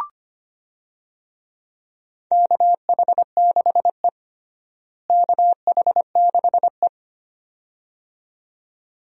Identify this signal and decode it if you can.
Morse code, heard as a keyed tone, about 25 wpm, 700 Hz. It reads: KH6E KH6E
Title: DXCC-100 Call Signs - 4 Characters - Repeated 2x - 25wpm